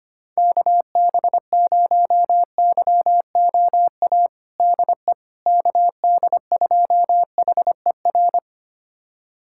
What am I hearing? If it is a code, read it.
Morse code, heard as a keyed tone, about 25 wpm, 700 Hz. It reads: KB0YOA DE KD2HER
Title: USA Callsign DE Callsign - 6 Characters - 25wpm